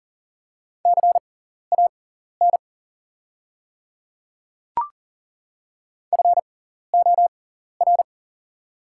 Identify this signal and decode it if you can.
Morse code, heard as a keyed tone, about 40 wpm, spaced as for 14 wpm, 690 Hz. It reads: CAN FOR
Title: 6x Character Spacing - Top 200 Words - 40wpm